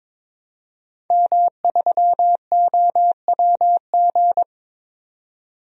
Morse code, heard as a keyed tone, about 21 wpm, 700 Hz. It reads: M3OWG
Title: DXCC-100 Call Signs - 5 Characters - 22wpm